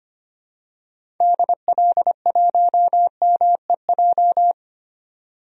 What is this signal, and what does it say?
Morse code, heard as a keyed tone, about 25 wpm, 700 Hz. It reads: DL1MEJ
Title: DXCC-100 Call Signs - 6 Characters - 25wpm